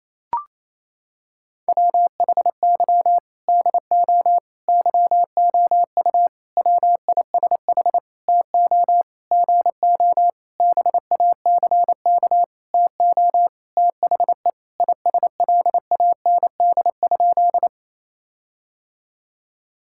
Morse code, heard as a keyed tone, about 27 wpm, 700 Hz. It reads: WHY DO YOU WISH TO GO BACK TO THE ISLAND?
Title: Sentences from Top 1600 Words 28wpm